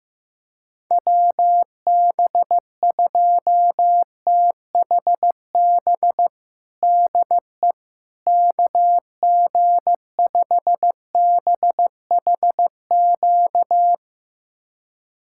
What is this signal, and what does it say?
Morse code, heard as a keyed tone, about 16 wpm, 700 Hz. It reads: WB2THB DE KG5BHQ
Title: USA Callsign DE Callsign 15wpm